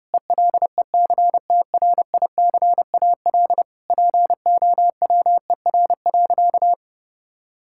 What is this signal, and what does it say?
Morse code, heard as a keyed tone, about 32 wpm, 700 Hz. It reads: ELECTRICAL POWER.
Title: Sentences from Top 1000 Words - Review Part 6 - 30wpm